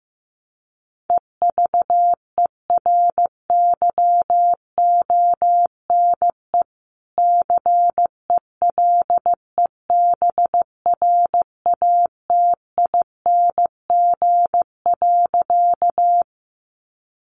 Morse code, heard as a keyed tone, about 15 wpm, 700 Hz. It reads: EVERYONE CELEBRATING.
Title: A Sound Of Thunder 15wpm